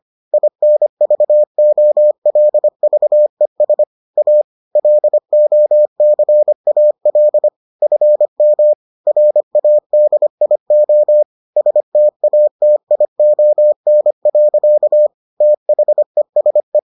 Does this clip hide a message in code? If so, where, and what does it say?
Yes: Morse code from 0.3 to 16.8 s, heard as a keyed tone, about 25 wpm, 600 Hz. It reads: INVOLVES A LOCAL FM RADIO STATION. THESE